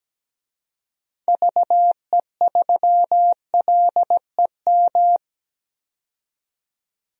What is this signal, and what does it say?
Morse code, heard as a keyed tone, about 17 wpm, 700 Hz. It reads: VE3LEM
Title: DXCC-100 Call Signs - 6 Characters - 17wpm